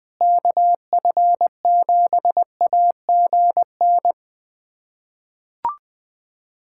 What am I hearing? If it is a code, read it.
Morse code, heard as a keyed tone, about 21 wpm, 700 Hz. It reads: KF7AGN
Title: USA Callsign DE Callsign - 6 Characters - 20wpm